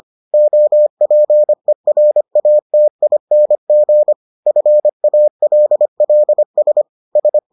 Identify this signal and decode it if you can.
Morse code, heard as a keyed tone, about 25 wpm, 600 Hz. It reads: OPERATING FALLS S